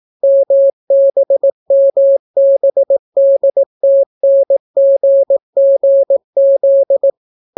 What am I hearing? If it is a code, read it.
Morse code, heard as a keyed tone, about 18 wpm, 550 Hz. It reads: MBMBDTNGGZ